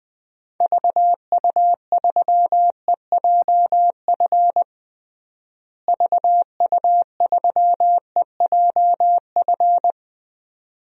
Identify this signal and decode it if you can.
Morse code, heard as a keyed tone, about 20 wpm, 700 Hz. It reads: VU3EJF VU3EJF
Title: DXCC-100 Call Signs - 6 Characters - Repeated 2x - 20wpm